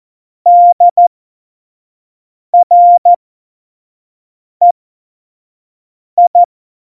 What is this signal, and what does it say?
Morse code, heard as a keyed tone, about 14 wpm, spaced as for 5 wpm, 700 Hz. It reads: DREI
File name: Morsekurs / Witze / Witz 04. Februar 2020, 25 F70